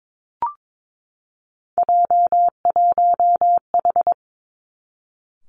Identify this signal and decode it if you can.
Morse code, heard as a keyed tone, about 22 wpm, 700 Hz. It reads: J1H
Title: Three Challenging Letter-Number Combinations 22wpm